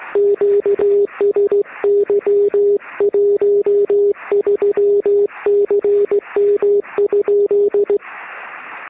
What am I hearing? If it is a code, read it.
Morse code, heard as a keyed tone, about 16 wpm, 400 Hz. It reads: QSY 13CM?
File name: sm7lcb / 1802z sm7dte 3cm